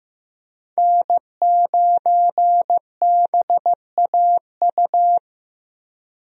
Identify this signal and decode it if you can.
Morse code, heard as a keyed tone, about 15 wpm, 700 Hz. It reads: N9BAU
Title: USA Call Signs - 5 Characters - Repeated 3x - 15wpm